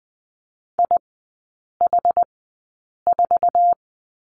Morse code, heard as a keyed tone, about 20 wpm, spaced as for 8 wpm, 690 Hz. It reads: IH4
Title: 081 Lesson 14e - Number 2 - Three Characters 20wpm